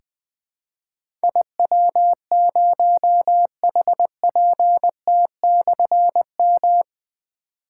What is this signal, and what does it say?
Morse code, heard as a keyed tone, about 20 wpm, 700 Hz. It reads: IW0HPT/M
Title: Difficult Call Signs 20wpm